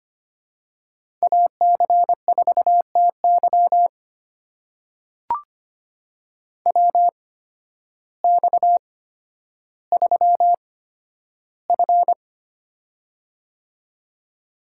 Morse code, heard as a keyed tone, about 25 wpm, spaced as for 12 wpm, 700 Hz. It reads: AC4TY WX3F
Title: USA Call Signs - 8x Character Spacing - 25wpm